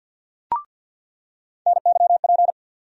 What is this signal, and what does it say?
Morse code, heard as a keyed tone, about 51 wpm, 700 Hz. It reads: NYP
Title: USA Call Sign Postfixes - 3 Letters 50wpm